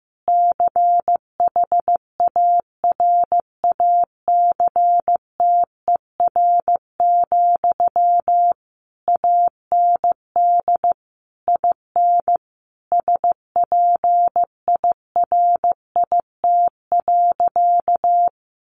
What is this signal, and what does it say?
Morse code, heard as a keyed tone, about 15 wpm, 700 Hz. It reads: CHARACTER, AND IN SPIRIT.
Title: Sentences from Top 1400 Words 15wpm